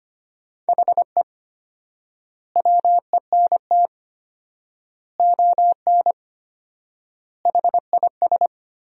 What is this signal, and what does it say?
Morse code, heard as a keyed tone, about 25 wpm, 700 Hz. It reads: HE WENT ON HIS
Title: 4x Word-Spacing - Sentences from Top 200 Words - 25wpm